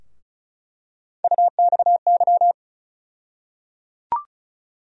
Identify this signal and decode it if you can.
Morse code, heard as a keyed tone, about 34 wpm, 700 Hz. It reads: UXY
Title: USA Call Sign Postfixes - 3 Letters 35wpm